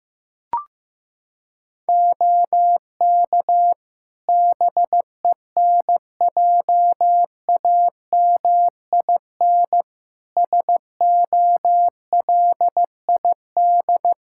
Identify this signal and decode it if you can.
Morse code, heard as a keyed tone, about 15 wpm, 700 Hz. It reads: OK BENJAMIN SOLID
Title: K1USN SST QSO Practice 15wpm